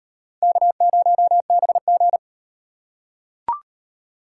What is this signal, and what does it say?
Morse code, heard as a keyed tone, about 38 wpm, 700 Hz. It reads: K0BG